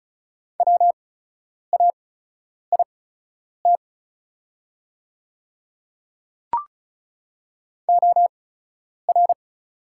Morse code, heard as a keyed tone, about 36 wpm, spaced as for 10 wpm, 700 Hz. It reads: WAIT OR
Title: 8x Character Spacing - Top 500 Words - 35wpm